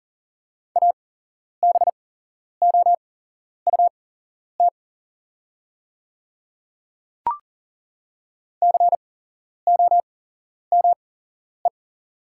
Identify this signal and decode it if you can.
Morse code, heard as a keyed tone, about 40 wpm, spaced as for 11 wpm, 690 Hz. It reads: ABOUT COME
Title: 8x Character Spacing - Top 500 Words - 40wpm